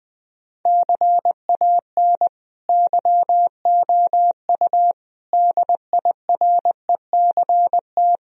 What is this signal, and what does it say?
Morse code, heard as a keyed tone, about 19 wpm, 700 Hz. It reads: CAN YOU DIRECT
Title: Sentences from Top 1000 Words - Review Part 2 - 20wpm